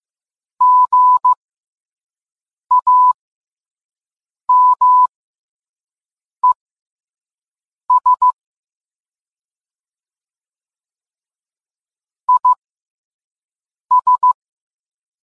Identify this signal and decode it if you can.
Morse code, heard as a keyed tone, about 15 wpm, spaced as for 5 wpm, 1000 Hz. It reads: GAMES IS
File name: posts / gmt-games-50-off-coupon-if-you-can-read-it